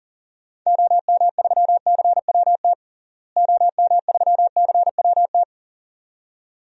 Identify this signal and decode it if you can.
Morse code, heard as a keyed tone, about 37 wpm, 700 Hz. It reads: OM3CWT OM3CWT
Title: DXCC-100 Call Signs - 6 Characters - Repeated 2x - 40wpm